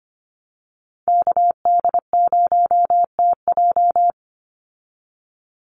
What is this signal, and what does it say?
Morse code, heard as a keyed tone, about 25 wpm, 700 Hz. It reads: KD0TJ
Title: USA Call Signs 25wpm